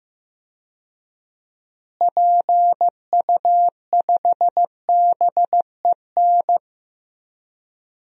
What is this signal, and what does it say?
Morse code, heard as a keyed tone, about 15 wpm, 700 Hz. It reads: PU5BEN
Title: DXCC-100 Call Signs - 6 Characters - Repeated 3x - 15wpm